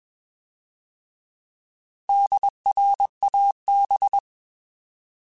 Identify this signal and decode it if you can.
Morse code, heard as a keyed tone, about 21 wpm, 800 Hz. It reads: DRAB
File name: Alphabetical / Alphabetical mp3s / Ds @20